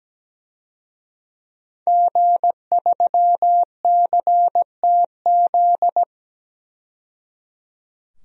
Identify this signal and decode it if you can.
Morse code, heard as a keyed tone, about 17 wpm, 700 Hz. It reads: G3CTZ
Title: DXCC-100 Call Signs - 5 Characters - 17wpm